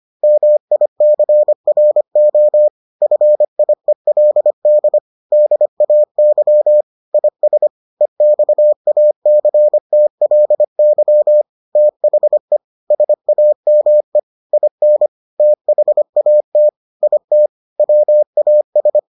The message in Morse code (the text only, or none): MICRO FIELD DAY IS EXACTLY THE SAME IN THAT IT WAS